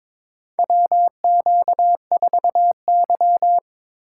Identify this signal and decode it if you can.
Morse code, heard as a keyed tone, about 22 wpm, 700 Hz. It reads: WQ4Y
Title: USA Call Signs - 4 Characters - 3x Character Spacing - 22wpm